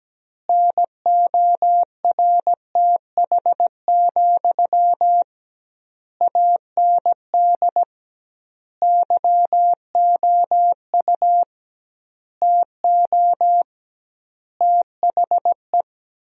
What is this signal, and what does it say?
Morse code, heard as a keyed tone, about 17 wpm, 700 Hz. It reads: NORTH, AND YOU TO THE
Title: 2x Word-Spacing - Sentences from Top 400 Words - 17wpm